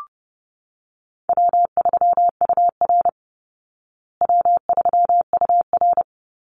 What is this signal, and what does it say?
Morse code, heard as a keyed tone, about 30 wpm, 700 Hz. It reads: W3UR W3UR
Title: Easy Call Signs Repeated 2x - 30wpm